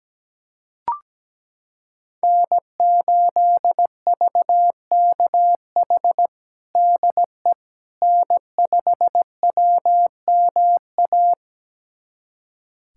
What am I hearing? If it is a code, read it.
Morse code, heard as a keyed tone, about 17 wpm, 700 Hz. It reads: N8VKH DE N5WMA
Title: USA Callsign DE Callsign - 5 Characters - 17wpm